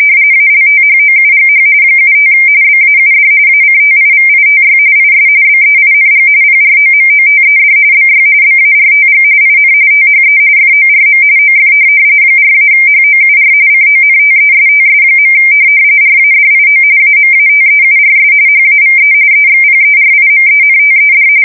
RTTY, about 45.45 baud. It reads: ALTHOUGH 75 BAUD ACTIVITY IS BEING PROMOTED BY BARTG IN THE FORM OF 4-HOUR CONTESTS. RADIO AMATEURS HAVE EXPERIMENTED
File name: fsk170 / teletype-RTTY